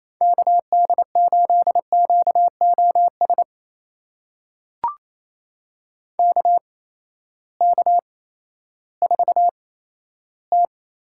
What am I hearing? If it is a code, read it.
Morse code, heard as a keyed tone, about 28 wpm, spaced as for 14 wpm, 700 Hz. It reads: KD8QOS KK4T
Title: USA Call Signs - 8x Character Spacing - 28wpm